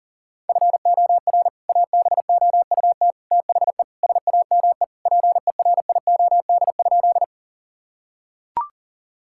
Morse code, heard as a keyed tone, about 37 wpm, 700 Hz. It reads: FOR ABOUT THE SAME PERIOD?
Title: Sentences from Top 1000 Words - Review Part 5 - 40wpm